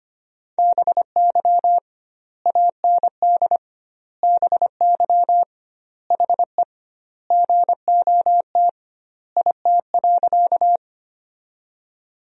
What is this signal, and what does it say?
Morse code, heard as a keyed tone, about 25 wpm, 700 Hz. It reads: BY AND BY HE GOT IT.
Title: 2x Word-Spacing - Sentences from Top 300 Words - 25wpm